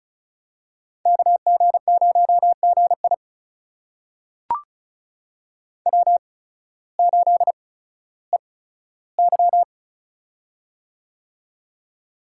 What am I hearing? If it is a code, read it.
Morse code, heard as a keyed tone, about 36 wpm, spaced as for 16 wpm, 700 Hz. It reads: KG0GI W8EY
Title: USA Call Signs - 8x Character Spacing - 35wpm